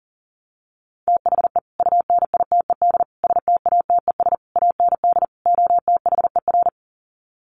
Morse code, heard as a keyed tone, about 40 wpm, 700 Hz. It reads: THE UNITED STATES AND OTHER
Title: Sets of 5 Words Encore 40wpm